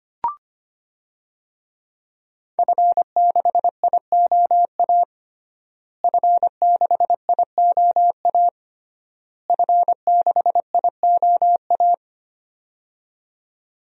Morse code, heard as a keyed tone, about 25 wpm, 700 Hz. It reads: F6IOA F6IOA F6IOA
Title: DXCC-100 Call Signs - 5 Characters - Repeated 3x - 25wpm